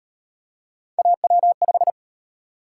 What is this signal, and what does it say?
Morse code, heard as a keyed tone, about 37 wpm, 700 Hz. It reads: AW5